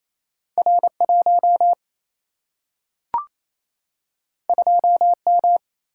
Morse code, heard as a keyed tone, about 28 wpm, 700 Hz. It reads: R1 2M
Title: Single-Letter Single-Number 28wpm